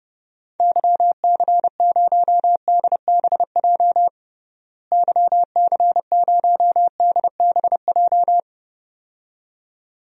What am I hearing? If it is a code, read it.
Morse code, heard as a keyed tone, about 30 wpm, 700 Hz. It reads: YC0DBJ YC0DBJ
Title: DXCC-100 Call Signs - 6 Characters - Repeated 2x - 30wpm